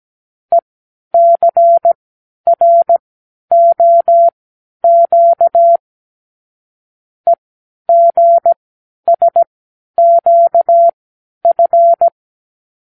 Morse code, heard as a keyed tone, about 17 wpm, spaced as for 10 wpm, 680 Hz. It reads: ECROQ EGSQF
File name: k7qo / Lesson 36 (A-X)